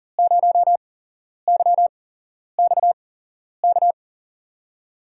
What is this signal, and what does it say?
Morse code, heard as a keyed tone, about 42 wpm, spaced as for 11 wpm, 700 Hz. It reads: 0YXK
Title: USA Call Signs - 6 Characters - 8x Character Spacing - 40wpm